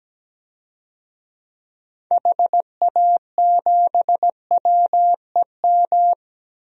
Morse code, heard as a keyed tone, about 17 wpm, 700 Hz. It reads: HA7WEM
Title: DXCC-100 Call Signs - 6 Characters - 17wpm